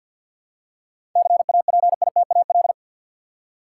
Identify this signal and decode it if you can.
Morse code, heard as a keyed tone, about 46 wpm, 700 Hz. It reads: CAPITAL
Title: Seven Letter Words - 50wpm